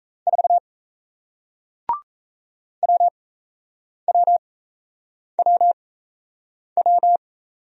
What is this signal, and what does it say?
Morse code, heard as a keyed tone, about 38 wpm, spaced as for 8 wpm, 700 Hz. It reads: 4 WWWW